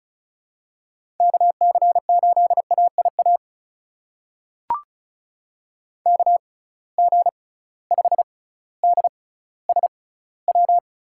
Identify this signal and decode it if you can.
Morse code, heard as a keyed tone, about 36 wpm, spaced as for 18 wpm, 700 Hz. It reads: KC8AIA KG5DSW
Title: USA Call Signs - 6 Characters - 6x Character Spacing - 35wpm